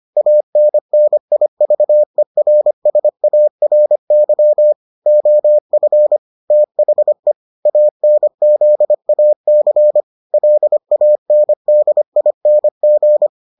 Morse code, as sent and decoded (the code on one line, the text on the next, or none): .- -. -. .. ...- . .-. ... .- .-. -.-- / --- ..-. / - .... / .- -. --.. .- -.-. / .-.. .- -. -.. .. -. --.
ANNIVERSARY OF THE ANZAC LANDING